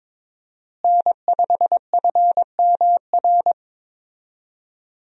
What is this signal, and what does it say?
Morse code, heard as a keyed tone, about 23 wpm, 700 Hz. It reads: N5FMR
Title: USA Call Signs - 5 Characters - Repeated 3x - 22wpm